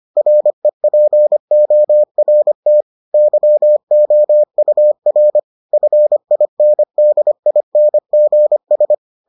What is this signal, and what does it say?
Morse code, heard as a keyed tone, about 25 wpm, 600 Hz. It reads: REPORT YOUR FINDINGS